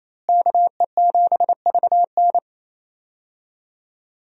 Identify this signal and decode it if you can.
Morse code, heard as a keyed tone, about 27 wpm, 700 Hz. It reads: KE7VN
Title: USA Call Signs - 5 Characters - Repeated 2x - 28wpm